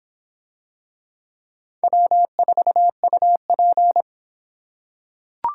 Morse code, heard as a keyed tone, about 26 wpm, 700 Hz. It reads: W4UP